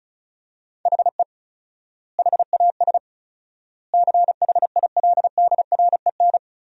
Morse code, heard as a keyed tone, about 36 wpm, 700 Hz. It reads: HE HAS CHILDREN
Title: 4x Word-Spacing - Sentences from Top 300 Words - 35wpm